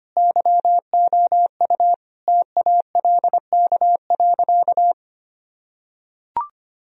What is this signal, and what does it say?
Morse code, heard as a keyed tone, about 25 wpm, 700 Hz. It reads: YOU TALK.
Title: Sentences from Top 1000 Words - Review Part 1 - 25wpm